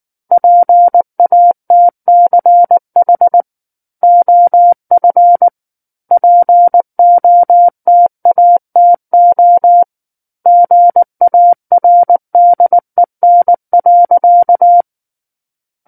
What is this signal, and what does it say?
Morse code, heard as a keyed tone, about 19 wpm, 700 Hz. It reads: PATCH OF POTATO GARDEN.
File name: Book One / 10-Chapter 10 - 19 WPM